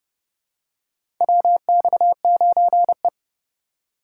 Morse code, heard as a keyed tone, about 30 wpm, 700 Hz. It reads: WX9E
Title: USA Call Signs - 4 Characters - 2x Character Spacing - 30wpm